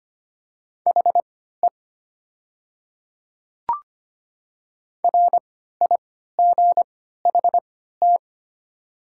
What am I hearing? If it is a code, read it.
Morse code, heard as a keyed tone, about 25 wpm, spaced as for 14 wpm, 700 Hz. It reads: HE RIGHT
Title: 3x Character Spacing - Top 300 Words - 25wpm